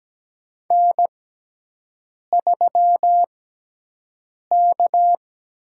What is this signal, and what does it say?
Morse code, heard as a keyed tone, about 17 wpm, spaced as for 6 wpm, 700 Hz. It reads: N3K
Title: USA Call Signs - 6x Character Spacing - 17wpm